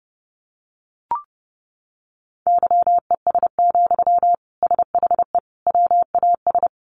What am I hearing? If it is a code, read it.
Morse code, heard as a keyed tone, about 28 wpm, 700 Hz. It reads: YES, SHE WAS
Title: Sentences from Top 500 Words 30wpm